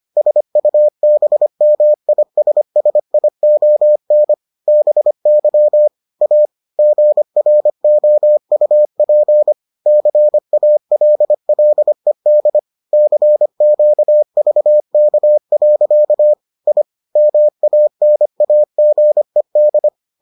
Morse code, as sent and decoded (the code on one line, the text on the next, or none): ... ..- -... -- .. ... ... .. --- -. / -... -.-- / .- / --. .-. --- ..- .--. / -.-. .- .-.. .-.. -.. / -.-. --.- ...- -.- .-.-.- / .. / -- .- -. .- --. -..
SUBMISSION BY A GROUP CALLED CQVK. I MANAGED